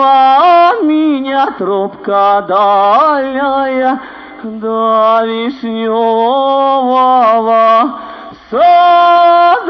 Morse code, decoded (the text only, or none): none